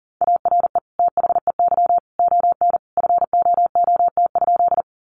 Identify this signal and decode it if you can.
Morse code, heard as a keyed tone, about 40 wpm, 700 Hz. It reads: ARE THEY ON FOOT?
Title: Sentences from Top 500 Words 40wpm